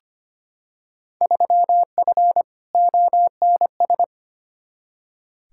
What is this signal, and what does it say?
Morse code, heard as a keyed tone, about 25 wpm, 700 Hz. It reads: 3F ONS
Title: Field Day Exchanges - 25wpm